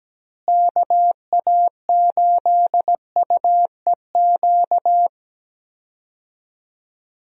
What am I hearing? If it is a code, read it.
Morse code, heard as a keyed tone, about 17 wpm, 700 Hz. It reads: KA8UEQ
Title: USA Call Signs - 6 Characters - Repeated 3x - 17wpm